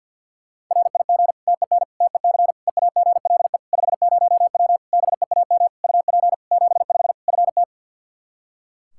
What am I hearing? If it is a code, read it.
Morse code, heard as a keyed tone, about 52 wpm, 700 Hz. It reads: RIG TEN TEC EAGLE 50W BEAM UP 75 FT
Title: Practice QSO - Exchange 2 - Info - 50wpm